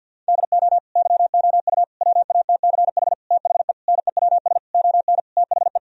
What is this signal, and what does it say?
Morse code, heard as a keyed tone, about 50 wpm, 700 Hz. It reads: DO YOU WATCH THE NEWS ON THE
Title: Sentences from Top 1000 Words - Review Part 4 - 50wpm